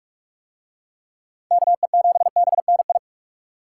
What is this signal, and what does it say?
Morse code, heard as a keyed tone, about 40 wpm, 700 Hz. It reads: KE7BNI